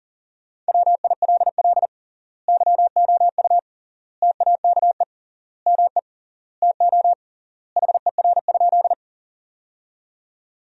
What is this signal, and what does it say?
Morse code, heard as a keyed tone, about 40 wpm, 700 Hz. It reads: WILL YOU TAKE ME TO HER?
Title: 3x Word-Spacing - Sentences from Top 200 Words - 40wpm